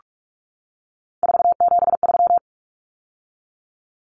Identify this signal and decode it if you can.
Morse code, heard as a keyed tone, about 44 wpm, 700 Hz. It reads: 473
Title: Numbers - 3-Digit 45wpm